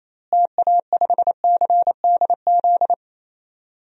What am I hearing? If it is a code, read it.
Morse code, heard as a keyed tone, about 28 wpm, 700 Hz. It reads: TA5CDZ
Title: DXCC-100 Call Signs - 6 Characters - Repeated 2x - 28wpm